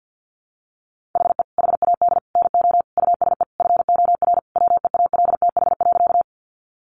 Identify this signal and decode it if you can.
Morse code, heard as a keyed tone, about 50 wpm, 700 Hz. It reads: HE HAD NO USE FOR WEALTH.
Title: Sentences from Top 1200 Words 50wpm